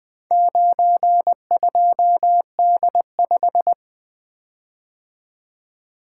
Morse code, heard as a keyed tone, about 21 wpm, 700 Hz. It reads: 92D5
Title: Four Letter-Number Combinations - 20wpm